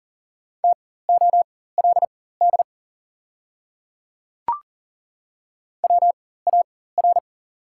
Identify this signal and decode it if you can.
Morse code, heard as a keyed tone, about 40 wpm, spaced as for 19 wpm, 700 Hz. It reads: TOLD WAR
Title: 4x Character Spacing - Top 500 Words - 40wpm